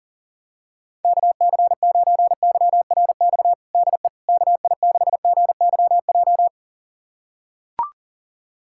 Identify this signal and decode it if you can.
Morse code, heard as a keyed tone, about 37 wpm, 700 Hz. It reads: KC9YRX DE KI6GYJ
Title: USA Callsign DE Callsign 40wpm